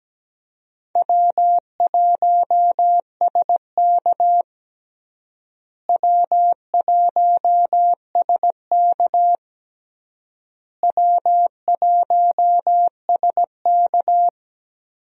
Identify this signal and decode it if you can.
Morse code, heard as a keyed tone, about 17 wpm, 700 Hz. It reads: W1SK W1SK W1SK
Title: USA Call Signs - 4 Characters - Repeated 3x - 17wpm